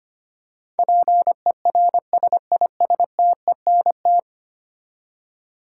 Morse code, heard as a keyed tone, about 25 wpm, 700 Hz. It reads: PERSISTENT
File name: practice / 25wpm / Ten Letter Words - 25wpm